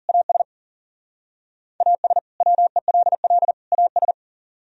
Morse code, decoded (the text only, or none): AS AS WELL AS